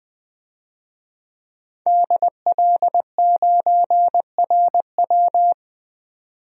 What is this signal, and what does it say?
Morse code, heard as a keyed tone, about 20 wpm, 700 Hz. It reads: DL9RW
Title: DXCC-100 Call Signs - 5 Characters - 20wpm